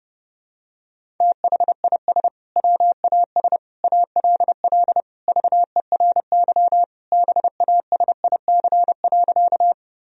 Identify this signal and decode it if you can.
Morse code, heard as a keyed tone, about 30 wpm, 700 Hz. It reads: THIS WAS ALL VERY BASIC.
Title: Sentences from Top 1000 Words - Review Part 3 - 30wpm